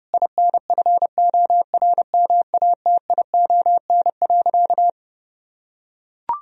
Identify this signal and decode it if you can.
Morse code, heard as a keyed tone, about 30 wpm, 700 Hz. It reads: INFORMATION.
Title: Sentences from Top 1000 Words - Review Part 3 - 30wpm